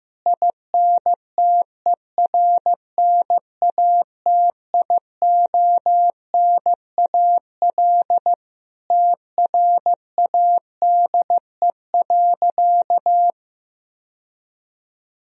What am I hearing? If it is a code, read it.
Morse code, heard as a keyed tone, about 16 wpm, 700 Hz. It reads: INTERNATIONAL TRADE.
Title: Sentences from Top 1000 Words - Review Part 4 - 15wpm